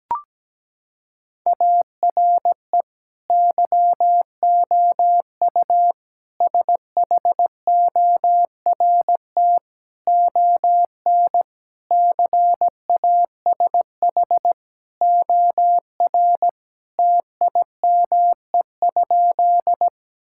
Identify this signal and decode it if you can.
Morse code, heard as a keyed tone, about 17 wpm, 700 Hz. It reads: ARE YOU SHORT ON CASH OR TIME?
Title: Sentences from Top 1000 Words - Review Part 5 - 17wpm